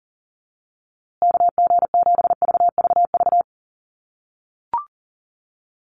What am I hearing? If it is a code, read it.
Morse code, heard as a keyed tone, about 40 wpm, 700 Hz. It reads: KG7VVV
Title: USA Call Signs Repeated 3x 40wpm